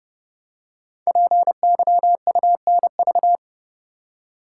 Morse code, heard as a keyed tone, about 29 wpm, 700 Hz. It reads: PYUNV
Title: Code Groups - 5 Letters - 30wpm